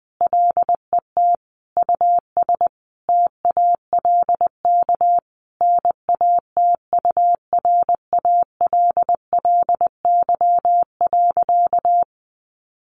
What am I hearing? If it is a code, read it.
Morse code, heard as a keyed tone, about 20 wpm, 700 Hz. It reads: LET US TALK NATURALLY.
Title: Sentences from Top 1300 Words 20wpm